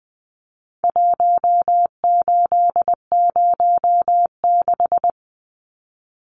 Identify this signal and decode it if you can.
Morse code, heard as a keyed tone, about 20 wpm, 700 Hz. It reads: 1806
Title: Numbers - 4-Digit 20wpm